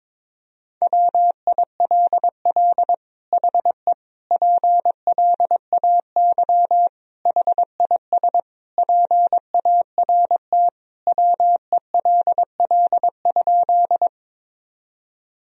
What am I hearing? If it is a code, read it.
Morse code, heard as a keyed tone, about 22 wpm, 700 Hz. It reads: WILL HE PLAY HIS PART WELL?
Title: Sentences from Top 1000 Words - Review Part 1 - 22wpm